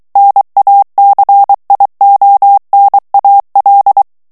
Morse code, heard as a keyed tone, about 23 wpm, 800 Hz. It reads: NACIONAL
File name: morse mp3 / ESPAGNOL TEXTE EN CLAIR 2 VITESSE 22 MOTS MINUTES